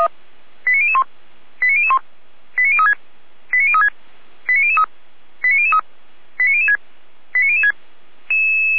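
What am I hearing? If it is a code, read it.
Five-tone selective call (ZVEI), heard as quick runs of five tones, tones of about 70 ms. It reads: 89001 89001 89036 89036 89003 89003 89006 89006
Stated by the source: ZVEI five-tone selective call received on an SDR, served by feuerwehr-orlamuende (Orlamünde fire brigade).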